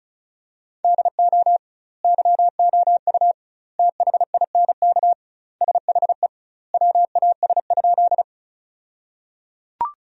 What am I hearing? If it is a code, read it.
Morse code, heard as a keyed tone, about 36 wpm, 700 Hz. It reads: DO YOU THINK SHE WAS?
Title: 2x Word-Spacing - Sentences from Top 200 Words - 35wpm